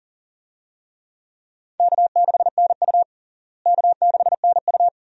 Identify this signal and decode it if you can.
Morse code, heard as a keyed tone, about 40 wpm, 700 Hz. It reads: K6NU K6NU
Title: USA Call Signs - 4 Characters - Repeated 2x - 40wpm